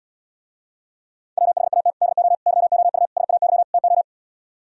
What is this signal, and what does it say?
Morse code, heard as a keyed tone, about 77 wpm, 700 Hz. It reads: WHAT DO YOU SAY TO